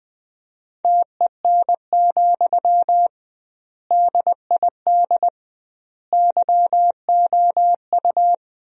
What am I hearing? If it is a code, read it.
Morse code, heard as a keyed tone, about 20 wpm, 700 Hz. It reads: TEN, DID YOU
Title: 2x Word-Spacing - Sentences from Top 400 Words - 20wpm